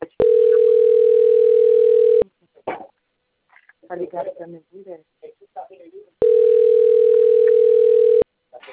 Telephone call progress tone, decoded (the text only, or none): ringback tone